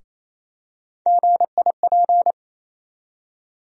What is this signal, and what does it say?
Morse code, heard as a keyed tone, about 28 wpm, 700 Hz. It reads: GIP
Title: USA Call Sign Postfixes - 3 Letters 28wpm